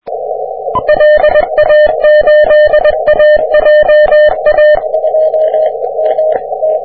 Morse code, heard as a keyed tone, about 21 wpm, 600 Hz. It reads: LA8AJA
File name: wp / vc1w 30m cw